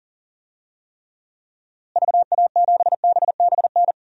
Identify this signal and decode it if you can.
Morse code, heard as a keyed tone, about 40 wpm, 700 Hz. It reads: VA7BBN
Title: DXCC-100 Call Signs - 6 Characters - Repeated 3x - 40wpm